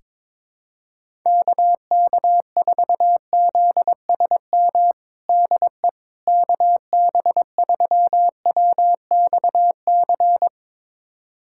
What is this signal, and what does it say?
Morse code, heard as a keyed tone, about 22 wpm, 700 Hz. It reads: KK4ZSM DE KB3WXC
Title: USA Callsign DE Callsign - 6 Characters - 22wpm